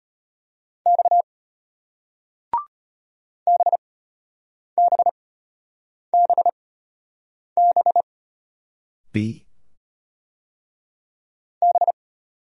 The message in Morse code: X BBBB B

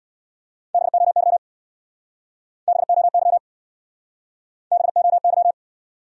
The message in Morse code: BOY BOY BOY

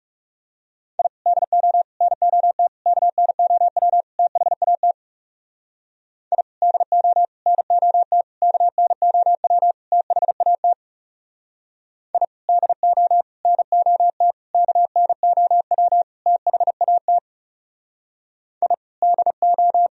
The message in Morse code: I DO NOT KNOW THAT I DO NOT KNOW THAT I DO NOT KNOW THAT I DO

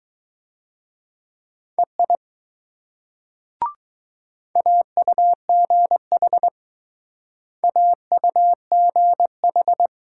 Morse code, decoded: EI AUGH AUGH

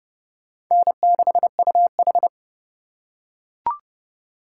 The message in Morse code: N6UH

